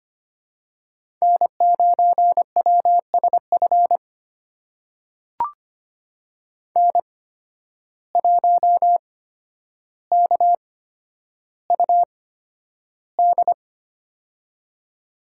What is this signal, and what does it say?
Morse code, heard as a keyed tone, about 25 wpm, spaced as for 11 wpm, 700 Hz. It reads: N9WSF N1KUD
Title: USA Call Signs - 5 Characters - 8x Character Spacing - 25wpm